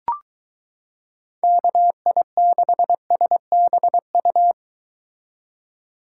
Morse code, heard as a keyed tone, about 23 wpm, 700 Hz. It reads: KI6SBU